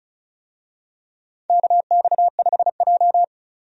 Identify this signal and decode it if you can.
Morse code, heard as a keyed tone, about 34 wpm, 700 Hz. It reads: KX5J
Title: USA Call Signs - 4 Characters - 35wpm